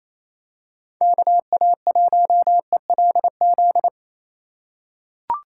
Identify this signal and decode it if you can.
Morse code, heard as a keyed tone, about 28 wpm, 700 Hz. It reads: KA1ELZ
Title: USA Call Signs - 6 Characters - 3x Character Spacing - 28wpm